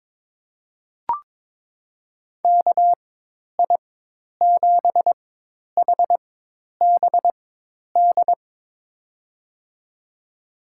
Morse code, heard as a keyed tone, about 22 wpm, spaced as for 10 wpm, 700 Hz. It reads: KI7HBD
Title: USA Call Signs - 6 Characters - 4x Character Spacing - 22wpm